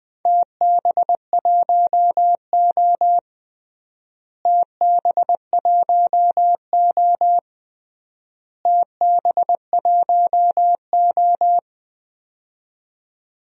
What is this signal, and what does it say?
Morse code, heard as a keyed tone, about 19 wpm, 700 Hz. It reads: TB1O TB1O TB1O
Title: DXCC-100 Call Signs - 4 Characters - Repeated 3x - 20wpm